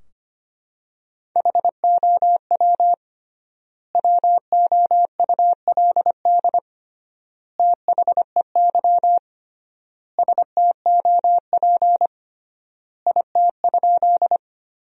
Morse code, heard as a keyed tone, about 25 wpm, 700 Hz. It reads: HOW WOULD THEY STOP IT?
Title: 3x Word-Spacing - Sentences from Top 300 Words - 25wpm